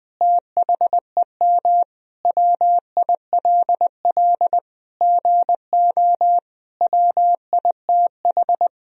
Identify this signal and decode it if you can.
Morse code, heard as a keyed tone, about 21 wpm, 700 Hz. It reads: THEM WILL GO WITH
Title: Sentences from Top 1000 Words - Review Part 3 - 20wpm